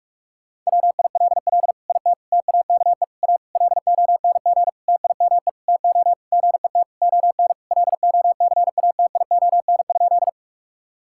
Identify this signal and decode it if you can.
Morse code, heard as a keyed tone, about 46 wpm, 700 Hz. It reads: WILL IT TAKE A LONG TIME TO GET ON LOCATION?
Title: Sentences from Top 1000 Words - Review Part 6 - 45wpm